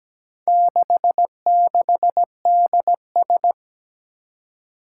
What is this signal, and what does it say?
Morse code, heard as a keyed tone, about 17 wpm, 700 Hz. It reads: 66DS
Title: Four Challenging Letter-Number Combinations 17wpm